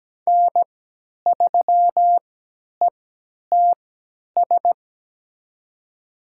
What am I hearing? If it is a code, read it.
Morse code, heard as a keyed tone, about 17 wpm, spaced as for 10 wpm, 700 Hz. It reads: N3ETS
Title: USA Call Signs - 5 Characters - 3x Character Spacing - 17wpm